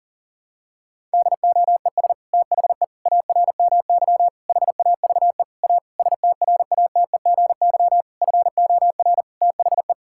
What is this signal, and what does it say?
Morse code, heard as a keyed tone, about 40 wpm, 700 Hz. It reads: DOES THE ARMY HAVE A STRATEGY FOR THE
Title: Sentences from Top 1000 Words - Review Part 5 - 40wpm